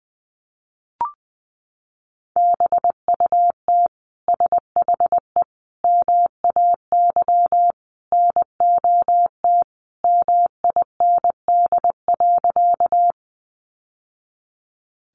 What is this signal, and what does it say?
Morse code, heard as a keyed tone, about 20 wpm, 700 Hz. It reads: BUT SHE MAY NOT MIND.
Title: Sentences from Top 600 Words 20wpm